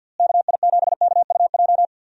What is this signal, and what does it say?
Morse code, heard as a keyed tone, about 50 wpm, 700 Hz. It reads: KI7KUJ
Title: USA Call Signs - 6 Characters - Repeated 3x - 50wpm